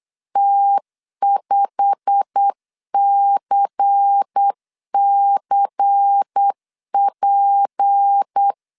Morse code, heard as a keyed tone, about 8 wpm, 800 Hz. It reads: T5CCP